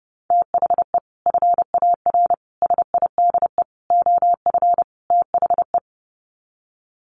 Morse code, heard as a keyed tone, about 30 wpm, 700 Hz. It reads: THE FAR SIDE OF THE
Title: Sets of 5 Words 30wpm